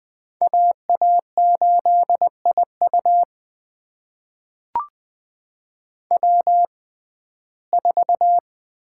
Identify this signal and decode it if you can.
Morse code, heard as a keyed tone, about 21 wpm, spaced as for 15 wpm, 700 Hz. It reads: AA8IU W4
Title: USA Call Signs - 5 Characters - 6x Character Spacing - 20wpm